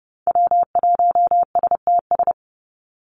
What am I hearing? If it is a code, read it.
Morse code, heard as a keyed tone, about 30 wpm, 700 Hz. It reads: W1STS
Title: USA Call Signs 30wpm